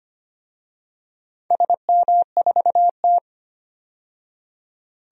Morse code, heard as a keyed tone, about 25 wpm, 700 Hz. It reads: SM4T
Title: DXCC-100 Call Signs - 4 Characters - 25wpm